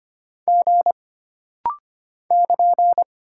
Morse code, heard as a keyed tone, about 25 wpm, 700 Hz. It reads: G <KN>
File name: Single Character (Letter:Number:Punctuation:Prosign) - Rapid-Fire - 25wpm